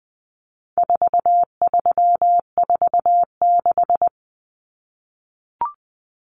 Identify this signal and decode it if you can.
Morse code, heard as a keyed tone, about 20 wpm, 700 Hz. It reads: 4346
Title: Numbers - 4-Digit 20wpm